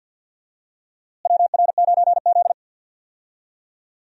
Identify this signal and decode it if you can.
Morse code, heard as a keyed tone, about 46 wpm, 700 Hz. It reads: WR9Z
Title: USA Call Signs - 4 Characters - Repeated 2x - 50wpm